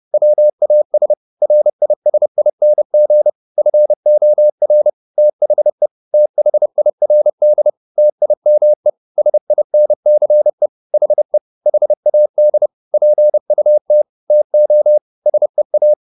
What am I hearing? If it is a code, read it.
Morse code, heard as a keyed tone, about 30 wpm, 600 Hz. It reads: WAS RISING FOR THE THIRD TIME SINCE HE HAD PUT TO SEA